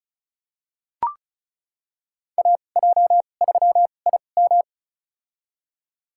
Morse code, heard as a keyed tone, about 36 wpm, spaced as for 25 wpm, 700 Hz. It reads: AJ3IM
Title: USA Call Signs - 5 Characters - 2x Character Spacing - 35wpm